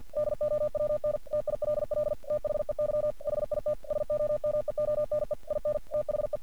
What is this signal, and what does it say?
Morse code, heard as a keyed tone, about 48 wpm, 600 Hz. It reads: DOWN TILL THEY HIT SOMEONE IN THE
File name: archived / CRSnet